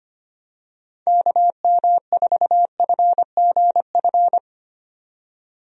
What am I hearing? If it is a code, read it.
Morse code, heard as a keyed tone, about 25 wpm, 700 Hz. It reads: KM4FGF
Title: USA Call Signs - 6 Characters - 25wpm